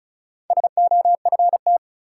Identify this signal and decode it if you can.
Morse code, heard as a keyed tone, about 36 wpm, 700 Hz. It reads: SOFT
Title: Sentences from Top 1500 Words - New Word Drill - 35wpm